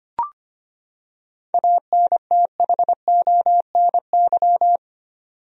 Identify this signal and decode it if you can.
Morse code, heard as a keyed tone, about 25 wpm, 700 Hz. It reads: ANTHONY
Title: USA Common Names - 25wpm